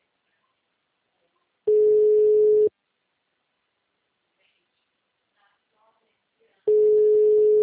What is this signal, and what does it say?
Ringback tone (Europe, 425 Hz), with speech.